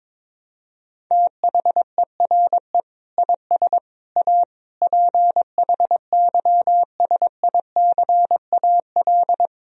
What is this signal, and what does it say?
Morse code, heard as a keyed tone, about 22 wpm, 700 Hz. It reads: THERE IS A PHYSICAL
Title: Sentences from Top 1000 Words - Review Part 4 - 22wpm